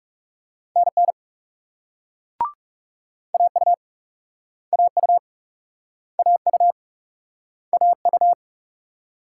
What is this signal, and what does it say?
Morse code, heard as a keyed tone, about 42 wpm, 700 Hz. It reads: NN AU AU AU AU